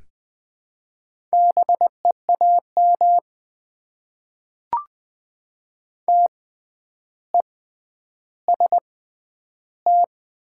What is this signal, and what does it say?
Morse code, heard as a keyed tone, about 20 wpm, spaced as for 10 wpm, 700 Hz. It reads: BEAM TEST
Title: Top 90 QSO Elements - 6x Character Spacing - 20wpm